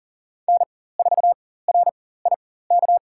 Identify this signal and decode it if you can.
Morse code, heard as a keyed tone, about 40 wpm, spaced as for 19 wpm, 700 Hz. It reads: N4RIK
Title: USA Call Signs - 5 Characters - 4x Character Spacing - 40wpm